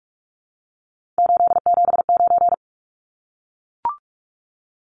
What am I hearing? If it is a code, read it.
Morse code, heard as a keyed tone, about 44 wpm, 700 Hz. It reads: 879
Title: Numbers - 3-Digit 45wpm